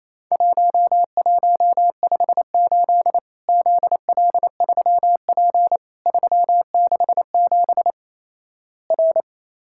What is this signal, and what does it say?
Morse code, heard as a keyed tone, about 28 wpm, 700 Hz. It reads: 1158 ZL3P 367 R
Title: Worked All Europe DX Contest - QTC Exchange - 28wpm